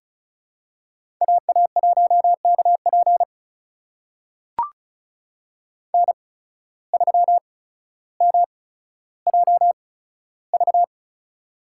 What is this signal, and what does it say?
Morse code, heard as a keyed tone, about 36 wpm, spaced as for 15 wpm, 700 Hz. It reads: AA1KP N3MJV
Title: USA Call Signs - 5 Characters - 8x Character Spacing - 35wpm